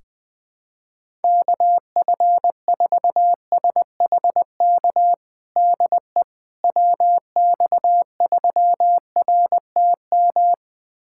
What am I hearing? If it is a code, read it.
Morse code, heard as a keyed tone, about 20 wpm, 700 Hz. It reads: KF4SHK DE WX3RTM
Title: USA Callsign DE Callsign - 6 Characters - 20wpm